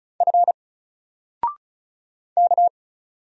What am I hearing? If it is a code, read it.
Morse code, heard as a keyed tone, about 35 wpm, 700 Hz. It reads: F K
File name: Single-Letter-Number-35wpm